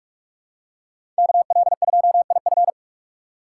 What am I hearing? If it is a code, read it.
Morse code, heard as a keyed tone, about 46 wpm, 700 Hz. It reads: KL2IF
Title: USA Call Signs - 5 Characters - 3x Character Spacing - 45wpm